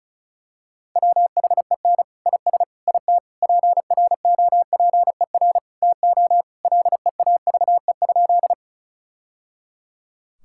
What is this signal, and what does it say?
Morse code, heard as a keyed tone, about 36 wpm, 700 Hz. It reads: WHEN IS IT PROPER TO LEAVE?
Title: Sentences from Top 1000 Words - Review Part 5 - 35wpm